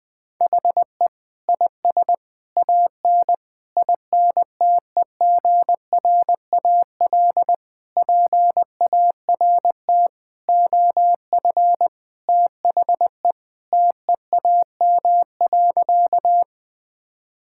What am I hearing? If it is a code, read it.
Morse code, heard as a keyed tone, about 20 wpm, 700 Hz. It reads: HE IS AN INTEGRAL PART OF THE TEAM.
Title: Sentences with 5-Word Phrases - 20wpm